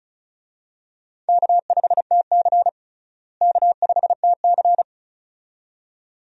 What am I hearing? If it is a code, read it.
Morse code, heard as a keyed tone, about 36 wpm, 700 Hz. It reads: K5TC K5TC
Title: USA Call Signs - 4 Characters - Repeated 2x - 35wpm